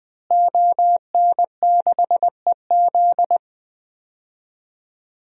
Morse code, heard as a keyed tone, about 20 wpm, 700 Hz. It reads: ON6EZ
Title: DXCC-100 Call Signs - 5 Characters - Repeated 3x - 20wpm